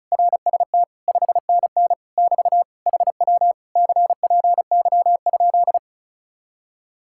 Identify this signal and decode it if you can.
Morse code, heard as a keyed tone, about 42 wpm, 700 Hz. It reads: RST 5NN <BT> HW CPY?
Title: Speed-Racing Common QSO Phrases - 35wpm